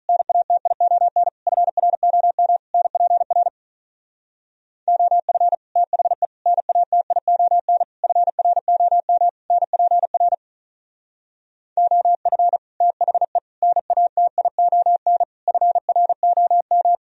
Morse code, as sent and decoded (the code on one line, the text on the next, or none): -. .- - .. --- -. / ..-. .-. --- -- / -. .--. .-. / --- ..-. / - .... / -. .- - .. --- -. / ..-. .-. --- -- / -. .--. .-. / --- ..-. / - .... / -. .- - .. --- -. / ..-. .-. --- --
NATION FROM NPR OF THE NATION FROM NPR OF THE NATION FROM